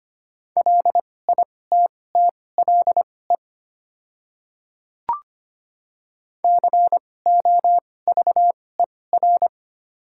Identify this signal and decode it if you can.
Morse code, heard as a keyed tone, about 25 wpm, spaced as for 18 wpm, 690 Hz. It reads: LITTLE COVER
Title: 2x Character Spacing - Top 300 Words - 25wpm